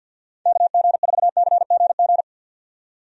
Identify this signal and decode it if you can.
Morse code, heard as a keyed tone, about 49 wpm, 700 Hz. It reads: KG4CGG
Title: USA Call Signs - 6 Characters - 50wpm